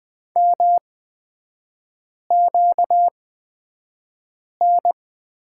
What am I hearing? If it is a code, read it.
Morse code, heard as a keyed tone, about 20 wpm, spaced as for 5 wpm, 700 Hz. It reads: MQN
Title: 236 Lesson 38e - Letter q - Three Characters 20wpm